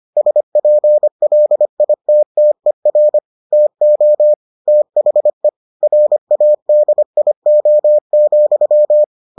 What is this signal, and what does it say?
Morse code, heard as a keyed tone, about 25 wpm, 600 Hz. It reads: SPLITTER TO THE RADIO,